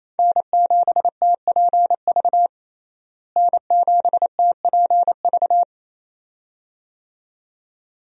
Morse code, heard as a keyed tone, about 28 wpm, 700 Hz. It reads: N7TPV N7TPV
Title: USA Call Signs - 5 Characters - Repeated 2x - 28wpm